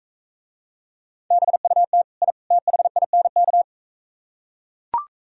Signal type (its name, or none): Morse code